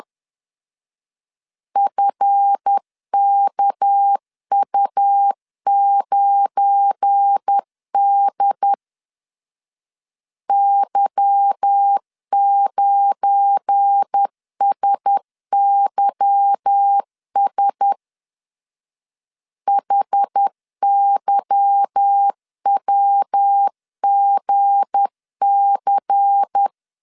Morse code, heard as a keyed tone, about 10 wpm, 800 Hz. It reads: FKU9D Y9SYS HYWGC
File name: LEZIONI / 8 YJ45 / 8 Y 10wpm 2615